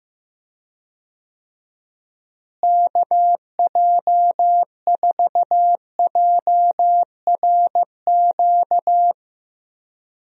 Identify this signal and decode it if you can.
Morse code, heard as a keyed tone, about 15 wpm, 700 Hz. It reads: KJ4JRQ
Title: USA Call Signs - 6 Characters - Repeated 2x - 15wpm